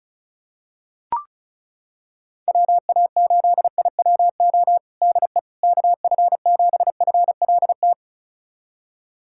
Morse code, heard as a keyed tone, about 36 wpm, 700 Hz. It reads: WA8IWO DE KF7FLT
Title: USA Callsign DE Callsign 35wpm